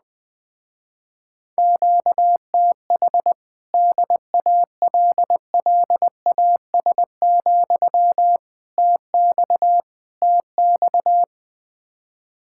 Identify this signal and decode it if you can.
Morse code, heard as a keyed tone, about 20 wpm, 700 Hz. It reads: QTH DALLAS, TX TX
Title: Speed-Racing Common QSO Phrases - 17wpm